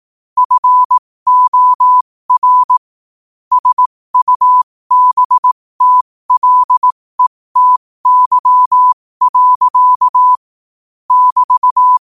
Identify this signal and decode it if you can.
Morse code, heard as a keyed tone, about 18 wpm, spaced as for 15 wpm, 1000 Hz. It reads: FOR SUBTLETY. <BT>
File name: qotdfiles / 15wpm